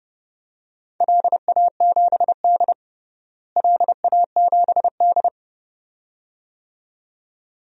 Morse code, heard as a keyed tone, about 32 wpm, 700 Hz. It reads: LA7D LA7D
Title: DXCC-100 Call Signs - 4 Characters - Repeated 2x - 30wpm